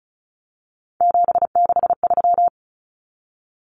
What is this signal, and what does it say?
Morse code, heard as a keyed tone, about 34 wpm, 700 Hz. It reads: 763
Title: Numbers - 3-Digit 35wpm